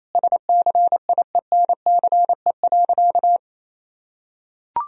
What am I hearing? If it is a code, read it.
Morse code, heard as a keyed tone, about 27 wpm, 700 Hz. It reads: SCIENCE.
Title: Sentences from Top 1000 Words - Review Part 2 - 28wpm